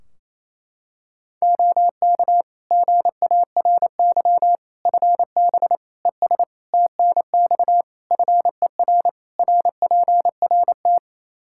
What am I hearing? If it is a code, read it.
Morse code, heard as a keyed tone, about 28 wpm, 700 Hz. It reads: OK GARY FB ES TNX FER RPRT
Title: Practice QSO - Exchange 2 - Info - 28wpm